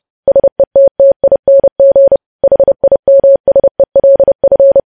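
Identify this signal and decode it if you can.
Morse code, heard as a keyed tone, about 30 wpm, 550 Hz. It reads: SETTING HIMSELF
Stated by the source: News headlines sent in Morse code at 30.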